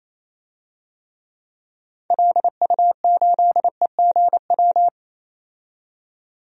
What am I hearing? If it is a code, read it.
Morse code, heard as a keyed tone, about 28 wpm, 700 Hz. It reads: LU8EGW
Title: DXCC-100 Call Signs - 6 Characters - 28wpm